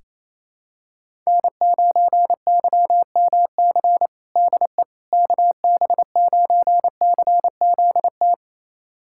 Morse code, heard as a keyed tone, about 28 wpm, 700 Hz. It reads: N9YMC DE KB9CZT
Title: USA Callsign DE Callsign 28wpm